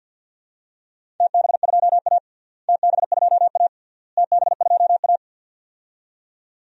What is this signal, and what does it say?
Morse code, heard as a keyed tone, about 52 wpm, 700 Hz. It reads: TB2A TB2A TB2A
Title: DXCC-100 Call Signs - 4 Characters - Repeated 3x - 50wpm